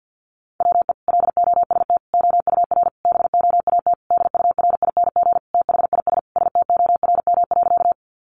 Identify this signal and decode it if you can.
Morse code, heard as a keyed tone, about 50 wpm, 700 Hz. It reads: WE LOST OUR BOAT DURING THIS STORM.
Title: Sentences from Top 1500 Words 50wpm